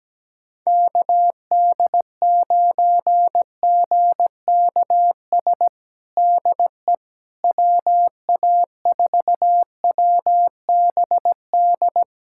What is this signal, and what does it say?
Morse code, heard as a keyed tone, about 17 wpm, 700 Hz. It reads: KD9GKS DE WA4WBD
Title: USA Callsign DE Callsign - 6 Characters - 17wpm